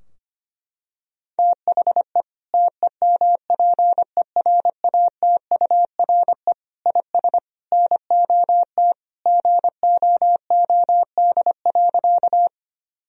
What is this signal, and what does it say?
Morse code, heard as a keyed tone, about 25 wpm, 700 Hz. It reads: THE TEMPERATURE IS NOT GOOD.
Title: Sentences from Top 1000 Words - Review Part 4 - 25wpm